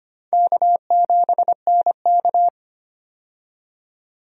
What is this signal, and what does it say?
Morse code, heard as a keyed tone, about 25 wpm, 700 Hz. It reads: K7NK
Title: USA Call Signs - 4 Characters - 25wpm